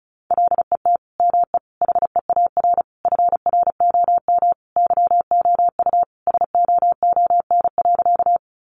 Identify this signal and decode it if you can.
Morse code, heard as a keyed tone, about 34 wpm, 700 Hz. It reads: LET ME HEAR FROM YOU SOON.
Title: Sentences from Top 400 Words 35wpm